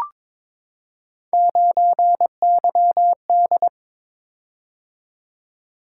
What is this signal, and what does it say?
Morse code, heard as a keyed tone, about 22 wpm, 700 Hz. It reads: 9YD
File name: practice / 22wpm / Three Letter-Number Combinations - 22wpm